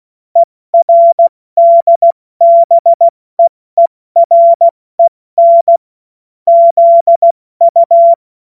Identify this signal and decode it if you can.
Morse code, heard as a keyed tone, about 16 wpm, 680 Hz. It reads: ERDBEEREN ZU
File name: Witz-16-September-2025-70-F80